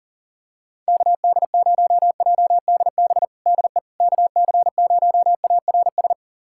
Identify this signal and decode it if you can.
Morse code, heard as a keyed tone, about 37 wpm, 700 Hz. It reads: KD0JDB DE KC0ARS
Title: USA Callsign DE Callsign - 6 Characters - 40wpm